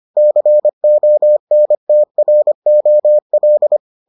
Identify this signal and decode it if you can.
Morse code, heard as a keyed tone, about 25 wpm, 600 Hz. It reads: CONTROL